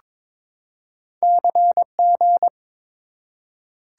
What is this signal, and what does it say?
Morse code, heard as a keyed tone, about 22 wpm, 700 Hz. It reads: CG
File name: practice / 22wpm / Two Letter-Number Combinations - 22wpm